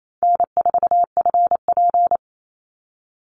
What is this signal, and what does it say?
Morse code, heard as a keyed tone, about 28 wpm, 700 Hz. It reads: N4FP
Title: Call Signs Repeated 2x - 28wpm